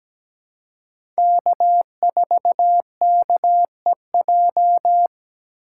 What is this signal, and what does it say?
Morse code, heard as a keyed tone, about 17 wpm, 700 Hz. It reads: K4KEJ
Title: USA Call Signs - 4x Character Spacing - 17wpm